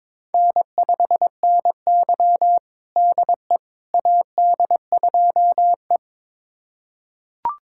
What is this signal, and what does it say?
Morse code, heard as a keyed tone, about 22 wpm, 700 Hz. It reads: N5NY DE AD2E
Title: USA Callsign DE Callsign - 4 Characters - 22wpm